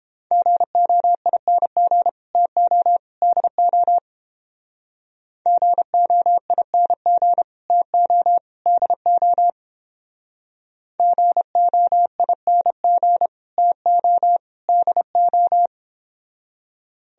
Morse code, heard as a keyed tone, about 29 wpm, 700 Hz. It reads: GOING TO DO GOING TO DO GOING TO DO